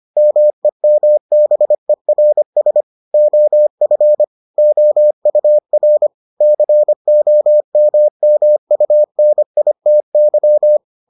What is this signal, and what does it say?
Morse code, heard as a keyed tone, about 25 wpm, 600 Hz. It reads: MEMBERS OF OUR COMMUNITY